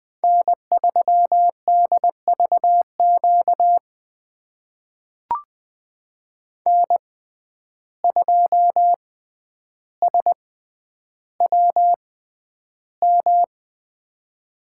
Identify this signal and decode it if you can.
Morse code, heard as a keyed tone, about 20 wpm, spaced as for 10 wpm, 700 Hz. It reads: N3DVQ N2SWM
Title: USA Call Signs - 5 Characters - 6x Character Spacing - 20wpm